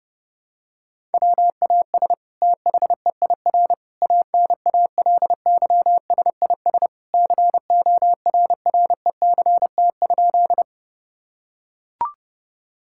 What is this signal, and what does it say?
Morse code, heard as a keyed tone, about 30 wpm, 700 Hz. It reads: WAS THEIR ANALYSIS CORRECT?
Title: Sentences from Top 1000 Words - Review Part 5 - 30wpm